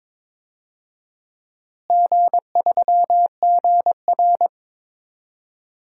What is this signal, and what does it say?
Morse code, heard as a keyed tone, about 22 wpm, 700 Hz. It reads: G3GR
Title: DXCC-100 Call Signs - 4 Characters - Repeated 2x - 22wpm